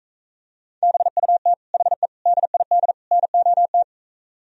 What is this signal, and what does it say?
Morse code, heard as a keyed tone, about 45 wpm, 700 Hz. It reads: BUT HE DID NOT